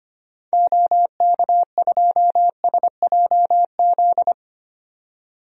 Morse code, heard as a keyed tone, about 25 wpm, 690 Hz. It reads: OK2SJZ
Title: DXCC-100 Call Signs - 6 Characters - 25wpm